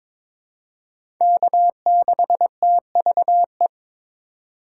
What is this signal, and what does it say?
Morse code, heard as a keyed tone, about 22 wpm, 690 Hz. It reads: K6TVE
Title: USA Call Signs - 5 Characters - 2x Character Spacing - 22wpm